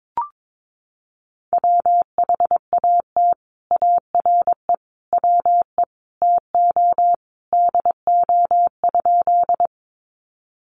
Morse code, heard as a keyed tone, about 22 wpm, 700 Hz. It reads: WHAT ARE WE TO DO?
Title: Sentences from Top 100 Words 22wpm